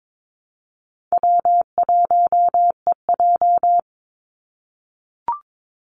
Morse code, heard as a keyed tone, about 22 wpm, 700 Hz. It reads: W1EJ
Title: NA SOTA Activator Callsigns 22wpm